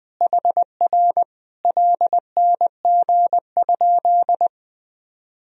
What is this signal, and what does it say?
Morse code, heard as a keyed tone, about 19 wpm, 700 Hz. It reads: HR LNG?
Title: Sentences from Top 200 Words and Abbreviations 20wpm